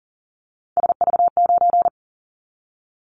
Morse code, heard as a keyed tone, about 42 wpm, 700 Hz. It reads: SV9
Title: Callsign Prefixes for Top 100 DXCC - 40wpm